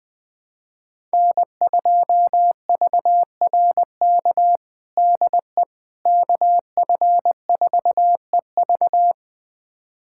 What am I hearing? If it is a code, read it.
Morse code, heard as a keyed tone, about 20 wpm, 700 Hz. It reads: N2VRK DE KF4EV
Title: USA Callsign DE Callsign - 5 Characters - 20wpm